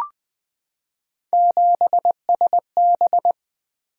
Morse code, heard as a keyed tone, about 20 wpm, 700 Hz. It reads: 7SB
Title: Three Letter-Number Combinations - 20wpm